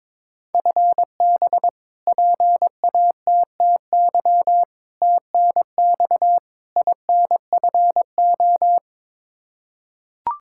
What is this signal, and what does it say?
Morse code, heard as a keyed tone, about 22 wpm, 700 Hz. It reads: FB PATTY TNX INFO
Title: K1USN SST QSO Practice 22wpm